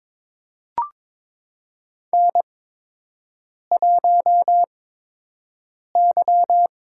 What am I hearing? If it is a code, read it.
Morse code, heard as a keyed tone, about 22 wpm, spaced as for 6 wpm, 700 Hz. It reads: N1Y
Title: USA Call Signs - 5 Characters - 8x Character Spacing - 22wpm